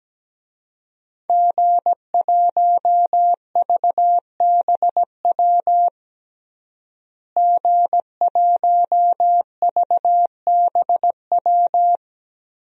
Morse code, heard as a keyed tone, about 17 wpm, 700 Hz. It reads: G1VBW G1VBW
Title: DXCC-100 Call Signs - 5 Characters - Repeated 2x - 17wpm